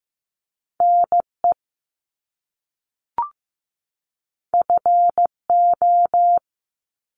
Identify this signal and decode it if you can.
Morse code, heard as a keyed tone, about 15 wpm, 700 Hz. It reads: NE FO
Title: Top Two Letter Combinations in Words 15wpm